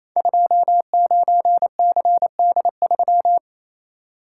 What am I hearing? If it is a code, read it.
Morse code, heard as a keyed tone, about 29 wpm, 700 Hz. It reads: 29CD3
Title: Code Groups - 5 Letter:Numbers - 28wpm